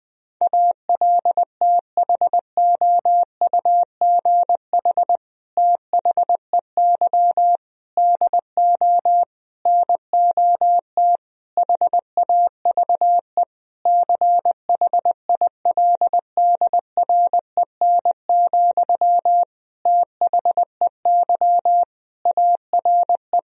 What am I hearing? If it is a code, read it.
Morse code, heard as a keyed tone, about 20 wpm, 700 Hz. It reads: ALTHOUGH THEY DO NOT HAVE CHILDREN, THEY ARE
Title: Sentences from Top 1000 Words - Review Part 3 - 20wpm